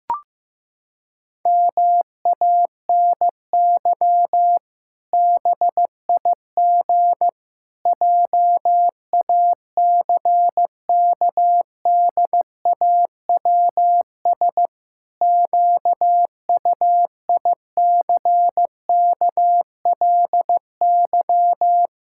Morse code, heard as a keyed tone, about 15 wpm, 700 Hz. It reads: MANY BIG JACKDAWS QUICKLY
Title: Pangram Challenge v2 - 15wpm